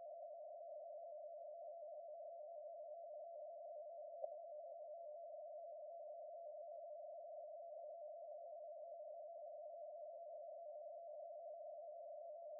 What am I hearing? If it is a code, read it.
Two-tone teleprinter signal, about 80 Hz shift.